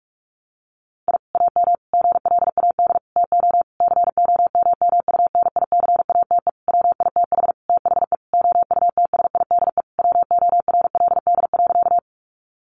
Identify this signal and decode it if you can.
Morse code, heard as a keyed tone, about 44 wpm, 700 Hz. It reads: I AM GLAD TO COMMUNICATE WITH THE OUTSIDE WORLD.
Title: Sentences from Top 1300 Words 45wpm